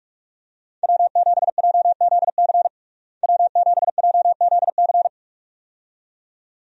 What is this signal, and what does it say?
Morse code, heard as a keyed tone, about 46 wpm, 700 Hz. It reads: W7JZC W7JZC
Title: USA Call Signs - 5 Characters - Repeated 2x - 45wpm